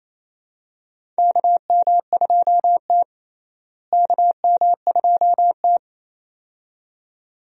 Morse code, heard as a keyed tone, about 28 wpm, 700 Hz. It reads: KM2T KM2T
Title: USA Call Signs - 4 Characters - Repeated 2x - 28wpm